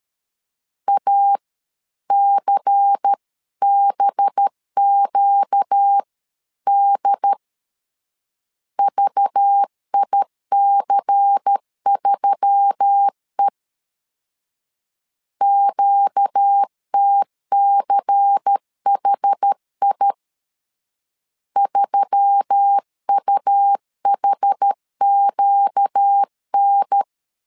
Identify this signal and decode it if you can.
Morse code, heard as a keyed tone, about 13 wpm, 800 Hz. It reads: ACBQD VIC3E QTCHI 3UHQN
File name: LEZIONI / 5 LBVC / 5 LBVC 12wpm 1955